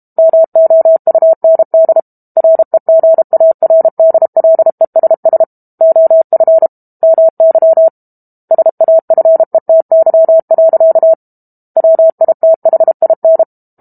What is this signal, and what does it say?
Morse code, heard as a keyed tone, about 32 wpm, 650 Hz. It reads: MOUND REGARDLESS OF MY SAFETY. WITHIN
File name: Book Two / 06-Chapter 6 - 32 WPM